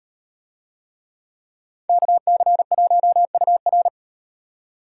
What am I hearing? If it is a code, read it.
Morse code, heard as a keyed tone, about 38 wpm, 700 Hz. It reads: KC1UR